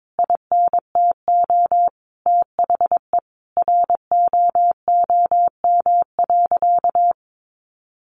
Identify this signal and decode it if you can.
Morse code, heard as a keyed tone, about 22 wpm, 700 Hz. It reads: INTO THE ROOM.
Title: Sentences from Top 600 Words 22wpm